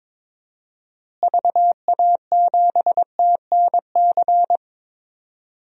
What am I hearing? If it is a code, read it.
Morse code, heard as a keyed tone, about 21 wpm, 700 Hz. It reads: VA7TNC
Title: DXCC-100 Call Signs - 6 Characters - Repeated 3x - 22wpm